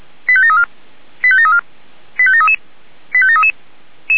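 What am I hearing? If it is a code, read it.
Five-tone selective call (ZVEI), heard as quick runs of five tones, tones of about 70 ms. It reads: 85734 85734 85733 85733
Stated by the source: ZVEI five-tone selective call received by SDR.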